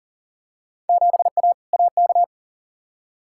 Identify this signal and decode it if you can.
Morse code, heard as a keyed tone, about 40 wpm, 700 Hz. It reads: 7A AK
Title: Field Day Exchanges Encore - 40wpm